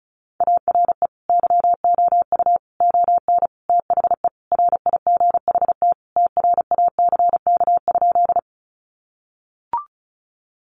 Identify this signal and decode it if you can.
Morse code, heard as a keyed tone, about 34 wpm, 700 Hz. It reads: ARE YOU ON THE RIGHT TRACK?
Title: Sentences from Top 1100 Words 35wpm